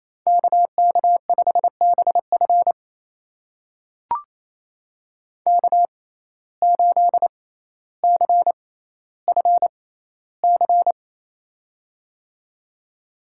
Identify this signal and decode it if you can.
Morse code, heard as a keyed tone, about 28 wpm, spaced as for 14 wpm, 700 Hz. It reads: KK5BF K8CFC
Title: USA Call Signs - 6x Character Spacing - 28wpm